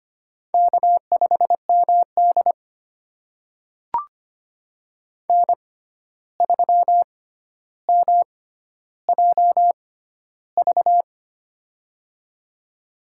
Morse code, heard as a keyed tone, about 25 wpm, spaced as for 12 wpm, 700 Hz. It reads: K5MD N3MJV
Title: USA Call Signs - 6x Character Spacing - 25wpm